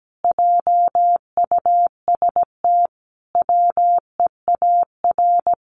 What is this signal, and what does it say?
Morse code, heard as a keyed tone, about 17 wpm, 700 Hz. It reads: JUST WEAR
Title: Sentences from Top 1500 Words 17wpm